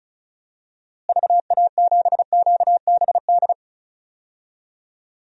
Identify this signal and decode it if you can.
Morse code, heard as a keyed tone, about 36 wpm, 700 Hz. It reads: VA7QBD
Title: DXCC-100 Call Signs - 6 Characters - Repeated 3x - 35wpm